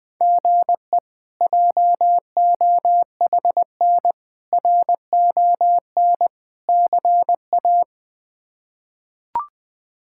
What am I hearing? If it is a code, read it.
Morse code, heard as a keyed tone, about 20 wpm, 700 Hz. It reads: GE JOHN RON CA
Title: K1USN SST QSO Practice - 20wpm